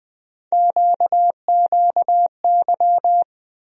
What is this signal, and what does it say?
Morse code, heard as a keyed tone, about 20 wpm, 700 Hz. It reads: QQY
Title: USA Call Sign Postfixes - 3 Letters 20wpm